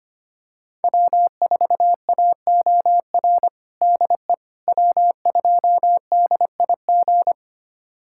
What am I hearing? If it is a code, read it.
Morse code, heard as a keyed tone, about 25 wpm, 700 Hz. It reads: W4AOR DE W2DIG
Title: USA Callsign DE Callsign - 5 Characters - 25wpm